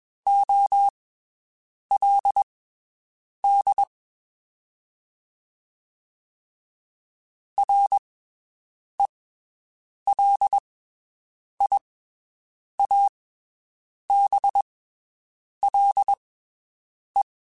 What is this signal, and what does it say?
Morse code, heard as a keyed tone, about 21 wpm, spaced as for 7 wpm, 800 Hz. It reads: OLD RELIABLE
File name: (spaced) / QSOs Spaced-mp3s / WD0FBT 1 @20 sp